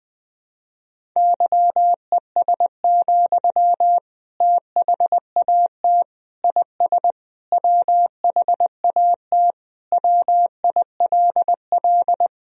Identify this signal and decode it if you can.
Morse code, heard as a keyed tone, about 20 wpm, 700 Hz. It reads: YES, THAT IS WHAT WILL
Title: Sentences from Top 500 Words and Abbreviations 20wpm